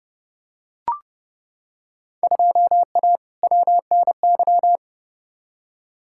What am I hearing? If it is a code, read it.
Morse code, heard as a keyed tone, about 32 wpm, 700 Hz. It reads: 2A WNY
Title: Field Day Exchanges - 30wpm